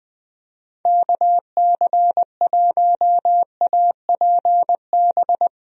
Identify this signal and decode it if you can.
Morse code, heard as a keyed tone, about 21 wpm, 700 Hz. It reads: KC1APB
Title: USA Call Signs - 6 Characters - 20wpm